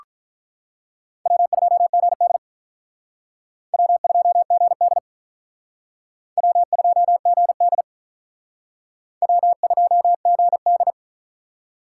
Morse code, heard as a keyed tone, about 44 wpm, 700 Hz. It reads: W2GD W2GD W2GD W2GD